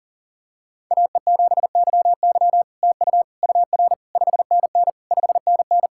Morse code, heard as a keyed tone, about 40 wpm, 700 Hz. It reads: AE7YY TU UR 5NN 5NN
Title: SOTA QSO Practice 40wpm